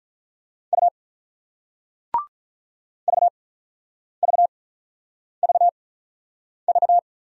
Morse code, heard as a keyed tone, about 47 wpm, spaced as for 9 wpm, 700 Hz. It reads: U VVVV